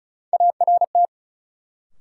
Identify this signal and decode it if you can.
Morse code, heard as a keyed tone, about 34 wpm, 700 Hz. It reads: ART
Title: Sentences from Top 600 Words - New Word Drill - 35wpm